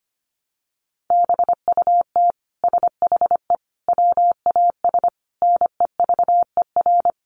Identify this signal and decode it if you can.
Morse code, heard as a keyed tone, about 25 wpm, 700 Hz. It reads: BUT SHE WAS NEVER
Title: Sentences from Top 700 Words 25wpm